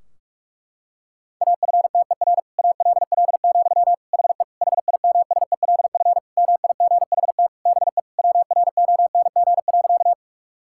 Morse code, heard as a keyed tone, about 46 wpm, 700 Hz. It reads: AFTER ALL, HE HIMSELF MIGHT BE WRONG.
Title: Sentences from Top 1600 Words 45wpm